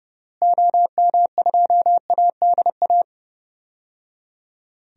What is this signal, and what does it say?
Morse code, heard as a keyed tone, about 30 wpm, 700 Hz. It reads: OM2ADA
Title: DXCC-100 Call Signs - 6 Characters - Repeated 3x - 30wpm